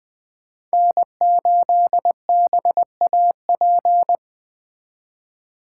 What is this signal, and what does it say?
Morse code, heard as a keyed tone, about 23 wpm, 700 Hz. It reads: N8BAP
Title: Speed Racing - USA 1x3 Call Signs 20wpm